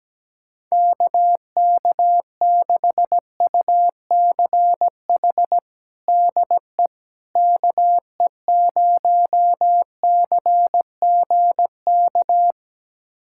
Morse code, heard as a keyed tone, about 17 wpm, 700 Hz. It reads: KK6UCH DE KE0CGK
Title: USA Callsign DE Callsign 17wpm